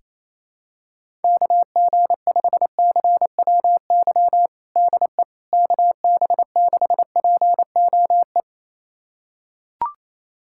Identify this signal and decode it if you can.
Morse code, heard as a keyed tone, about 27 wpm, 700 Hz. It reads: KG5CWY DE KB6POE
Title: USA Callsign DE Callsign - 6 Characters - 28wpm